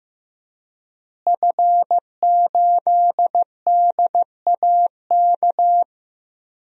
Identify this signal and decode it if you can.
Morse code, heard as a keyed tone, about 15 wpm, 700 Hz. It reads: F8DAK
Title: DXCC-100 Call Signs - 5 Characters - Repeated 3x - 15wpm